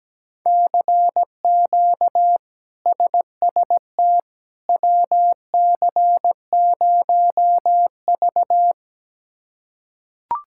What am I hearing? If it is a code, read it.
Morse code, heard as a keyed tone, about 17 wpm, 700 Hz. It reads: CQ SST WC0V
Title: K1USN SST QSO Practice - 17wpm